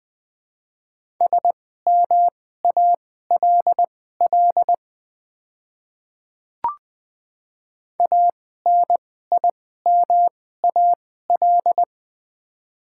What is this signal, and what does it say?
Morse code, heard as a keyed tone, about 20 wpm, spaced as for 14 wpm, 700 Hz. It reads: SMALL ANIMAL
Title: 2x Character Spacing - Top 200 Words - 20wpm